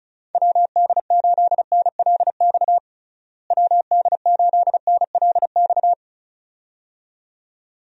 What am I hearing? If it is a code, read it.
Morse code, heard as a keyed tone, about 36 wpm, 700 Hz. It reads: WD8NLX WD8NLX
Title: USA Call Signs - 6 Characters - Repeated 2x - 35wpm